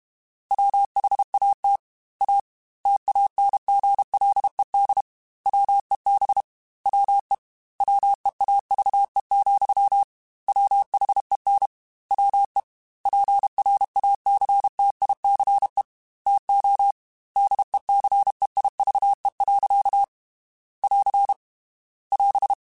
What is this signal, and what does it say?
Morse code, heard as a keyed tone, about 32 wpm, 800 Hz. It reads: WHAT A TANGLED WEB WE WEAVE, WHEN WE PRACTICE TO DECEIVE. <AR> <AS>
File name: Quotes / Quotes-mp3s / Deceive @30